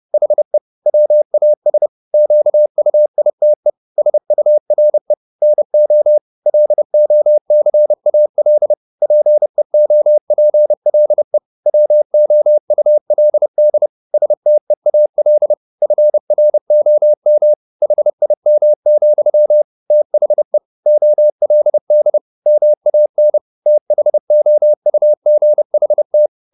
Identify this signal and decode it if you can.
Morse code, heard as a keyed tone, about 30 wpm, 600 Hz. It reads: HE WAS QUITE SURE NO LOCAL PEOPLE WOULD STEAL FROM HIM, THE OLD MAN THOUGHT
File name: the old man and the sea / the old man and the sea 1 30wpm